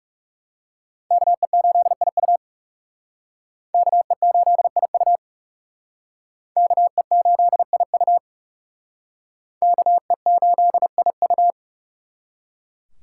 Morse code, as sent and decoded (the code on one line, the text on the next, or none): -.- . ---.. .. ..- / -.- . ---.. .. ..- / -.- . ---.. .. ..- / -.- . ---.. .. ..-
KE8IU KE8IU KE8IU KE8IU